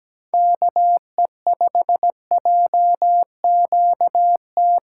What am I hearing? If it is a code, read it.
Morse code, heard as a keyed tone, about 17 wpm, 700 Hz. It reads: KE5JQT
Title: USA Call Signs - 6 Characters - 4x Character Spacing - 17wpm